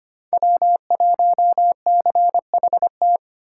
Morse code, heard as a keyed tone, about 33 wpm, 700 Hz. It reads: W1CHT